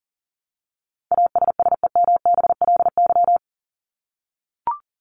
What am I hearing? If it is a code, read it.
Morse code, heard as a keyed tone, about 40 wpm, 700 Hz. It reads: ASSEMBLY
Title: LY Suffix - Top 100 Words with Base Words 40wpm